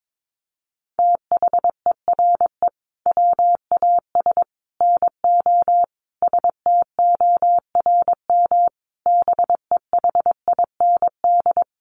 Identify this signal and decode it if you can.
Morse code, heard as a keyed tone, about 22 wpm, 700 Hz. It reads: THERE WAS NO STORM BEHIND
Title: Sentences from Top 1500 Words 22wpm